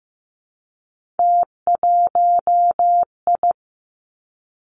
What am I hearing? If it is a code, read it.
Morse code, heard as a keyed tone, about 15 wpm, 700 Hz. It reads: T1I
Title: Three Challenging Letter-Number Combinations 15wpm